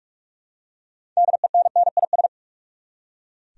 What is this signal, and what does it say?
Morse code, heard as a keyed tone, about 46 wpm, 700 Hz. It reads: DENNIS
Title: USA Common Names - 45wpm